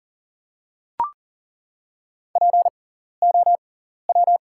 Morse code, heard as a keyed tone, about 40 wpm, spaced as for 14 wpm, 690 Hz. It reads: POW
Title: 6x Character Spacing - Top 500 Words - 40wpm